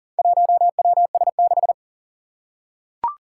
Morse code, heard as a keyed tone, about 40 wpm, 700 Hz. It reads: 1WS6
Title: Four Challenging Letter-Number Combinations 40wpm